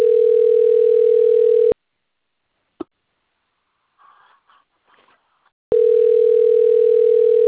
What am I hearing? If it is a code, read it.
Ringback tone (North America), 440 + 480 Hz.